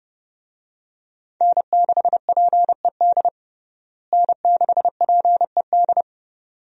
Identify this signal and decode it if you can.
Morse code, heard as a keyed tone, about 30 wpm, 700 Hz. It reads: N6PED N6PED
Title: USA Call Signs - 5 Characters - Repeated 2x - 30wpm